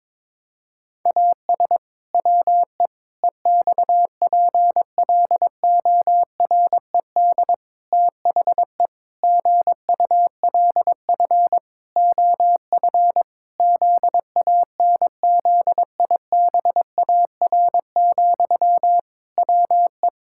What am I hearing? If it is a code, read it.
Morse code, heard as a keyed tone, about 22 wpm, 700 Hz. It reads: AS WE EXPLORED THE GULF OF ZANZIBAR, WE
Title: Pangram Challenge v2 - 22wpm